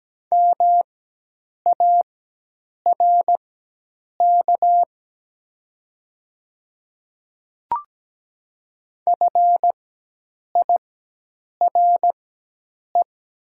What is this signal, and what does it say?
Morse code, heard as a keyed tone, about 17 wpm, spaced as for 8 wpm, 690 Hz. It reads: MARK FIRE
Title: 4x Character Spacing - Top 500 Words - 17wpm